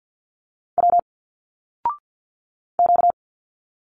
Morse code, heard as a keyed tone, about 49 wpm, 700 Hz. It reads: W Q
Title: Single Letters 50wpm